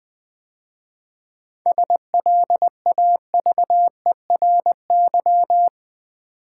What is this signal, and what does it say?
Morse code, heard as a keyed tone, about 20 wpm, 700 Hz. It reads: SLAVERY
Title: 7 Letter Words - First 3 Letters Given - 20wpm